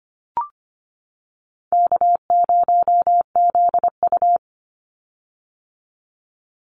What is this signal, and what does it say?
Morse code, heard as a keyed tone, about 25 wpm, 700 Hz. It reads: K0ZU
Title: Easy Call Signs - 25wpm